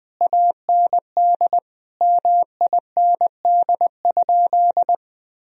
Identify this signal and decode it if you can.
Morse code, heard as a keyed tone, about 20 wpm, 700 Hz. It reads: AND MIND?
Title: Sentences from Top 1000 Words - Review Part 2 - 20wpm